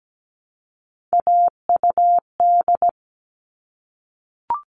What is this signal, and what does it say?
Morse code, heard as a keyed tone, about 17 wpm, 700 Hz. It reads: AUD
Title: All Cut Numbers - 3-Digit 17wpm